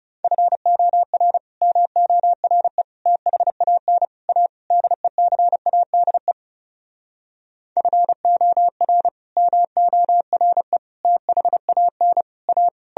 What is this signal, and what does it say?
Morse code, heard as a keyed tone, about 40 wpm, 700 Hz. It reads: FOR MORE THAN A DECADE FOR MORE THAN A